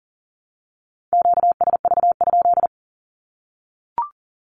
Morse code, heard as a keyed tone, about 40 wpm, 700 Hz. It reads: QSV?
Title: Q-Codes Repeated 3x 40wpm